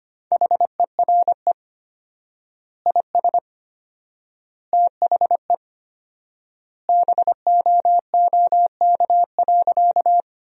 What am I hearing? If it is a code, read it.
Morse code, heard as a keyed tone, about 25 wpm, 700 Hz. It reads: HERE IS THE BOOK.
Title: 4x Word-Spacing - Sentences from Top 300 Words - 25wpm